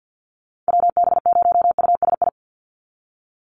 Morse code, heard as a keyed tone, about 48 wpm, 700 Hz. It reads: WB0USI
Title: NA SOTA Activator Callsigns 50wpm